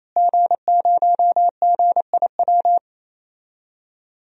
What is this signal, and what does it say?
Morse code, heard as a keyed tone, about 28 wpm, 700 Hz. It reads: G0GIW
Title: DXCC-100 Call Signs - 5 Characters - Repeated 3x - 28wpm